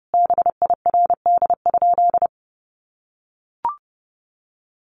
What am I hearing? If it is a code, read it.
Morse code, heard as a keyed tone, about 29 wpm, 700 Hz. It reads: BIRD?
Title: Sentences from Top 400 Words 30wpm